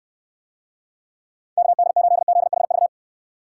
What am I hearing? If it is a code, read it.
Morse code, heard as a keyed tone, about 69 wpm, 700 Hz. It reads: KD8ZHV